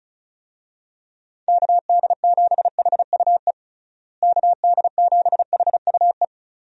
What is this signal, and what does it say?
Morse code, heard as a keyed tone, about 34 wpm, 700 Hz. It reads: KD7HUE KD7HUE
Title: USA Call Signs - 6 Characters - Repeated 3x - 35wpm